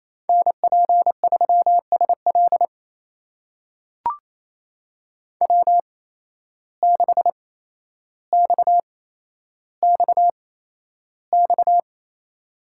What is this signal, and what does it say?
Morse code, heard as a keyed tone, about 27 wpm, spaced as for 12 wpm, 700 Hz. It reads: NP3SL W6XXX
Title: USA Call Signs - 5 Characters - 8x Character Spacing - 28wpm